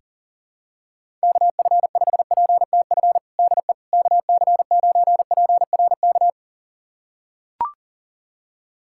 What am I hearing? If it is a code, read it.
Morse code, heard as a keyed tone, about 40 wpm, 700 Hz. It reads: KF5PTF DE KC9PRK
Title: USA Callsign DE Callsign - 6 Characters - 40wpm